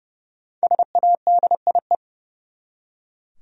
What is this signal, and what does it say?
Morse code, heard as a keyed tone, about 30 wpm, 700 Hz. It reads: SADIE
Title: USA Common Names - Max 5 Letters - 30wpm